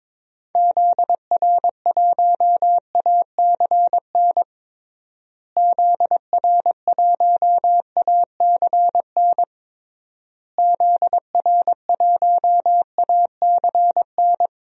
Morse code, heard as a keyed tone, about 22 wpm, 700 Hz. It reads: ZR1ACN ZR1ACN ZR1ACN
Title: DXCC-100 Call Signs - 6 Characters - Repeated 3x - 22wpm